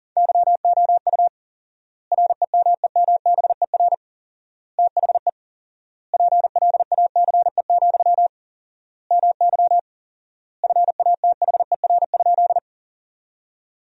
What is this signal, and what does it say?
Morse code, heard as a keyed tone, about 40 wpm, 700 Hz. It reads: YOU REMEMBER THE PLACE, MY FATHER?
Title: 4x Word-Spacing - Sentences from Top 400 Words - 40wpm